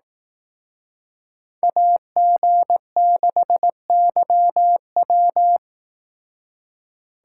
Morse code, heard as a keyed tone, about 18 wpm, 700 Hz. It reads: AG6YW
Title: Speed Racing - USA 2x2 Call Signs 15wpm